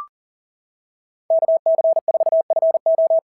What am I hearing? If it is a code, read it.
Morse code, heard as a keyed tone, about 39 wpm, 690 Hz. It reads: KC4FO
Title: SOTA QSO Practice 40wpm